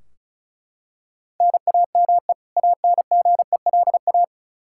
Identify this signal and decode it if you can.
Morse code, heard as a keyed tone, about 36 wpm, 700 Hz. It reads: NAME ANGELA
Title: Practice QSO - Exchange 1 - Beginning - 35wpm